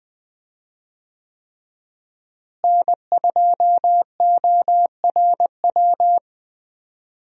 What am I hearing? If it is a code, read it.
Morse code, heard as a keyed tone, about 20 wpm, 700 Hz. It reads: N2ORW
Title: USA Call Signs - 5 Characters - Repeated 3x - 20wpm